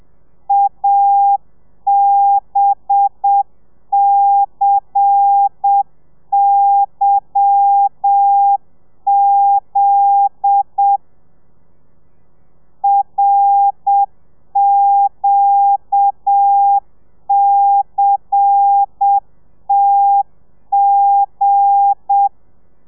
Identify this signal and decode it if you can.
Morse code, heard as a keyed tone, about 7 wpm, 800 Hz. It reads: ABCYZ RQCTG
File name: telegrafia / i2tfj / Lez10 30